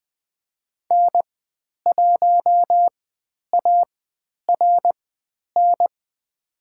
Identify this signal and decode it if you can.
Morse code, heard as a keyed tone, about 20 wpm, spaced as for 10 wpm, 700 Hz. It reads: N1ARN
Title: 102 Lesson 17g - Letter u - Call Signs 20wpm